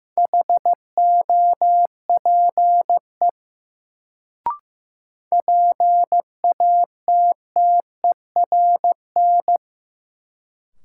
Morse code, heard as a keyed tone, about 15 wpm, 700 Hz. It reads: HOPE PATTERN
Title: Sentences from Top 1000 Words - New Word Drill - 15wpm